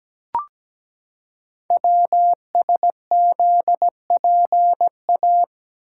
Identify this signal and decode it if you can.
Morse code, heard as a keyed tone, about 17 wpm, 700 Hz. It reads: WSZPA
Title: Code Groups - 5 Letters - 17wpm